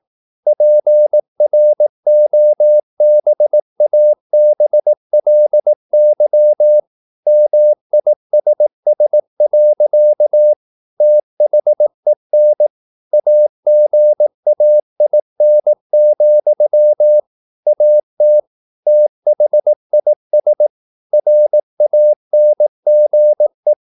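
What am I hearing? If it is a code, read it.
Morse code, heard as a keyed tone, about 18 wpm, 600 Hz. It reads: PROBABLY MISS. THEN AGAIN, AT THIS RANGE
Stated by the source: Morse practice at 18 wpm, English text, from CW med Gnister.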